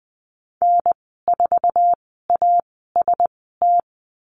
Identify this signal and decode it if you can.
Morse code, heard as a keyed tone, about 20 wpm, spaced as for 14 wpm, 700 Hz. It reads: N4AST
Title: 055 Lesson 10g - Letter r - Call Signs 20wpm